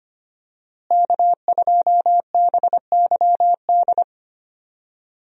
Morse code, heard as a keyed tone, about 25 wpm, 700 Hz. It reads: K2BYD
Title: USA Call Signs - 5 Characters - Repeated 2x - 25wpm